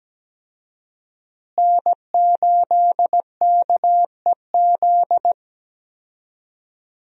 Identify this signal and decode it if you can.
Morse code, heard as a keyed tone, about 17 wpm, 700 Hz. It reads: N8KEZ
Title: USA Call Signs - 5 Characters - Repeated 2x - 17wpm